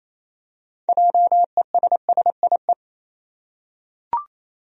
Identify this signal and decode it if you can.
Morse code, heard as a keyed tone, about 28 wpm, 700 Hz. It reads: JESSIE
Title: USA Common Names - 28wpm